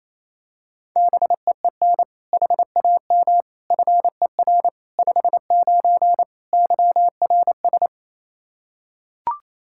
Morse code, heard as a keyed tone, about 27 wpm, 700 Hz. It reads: BEEN HAM FER 59 YRS
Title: Practice QSO - Exchange 3 - More Info - 28wpm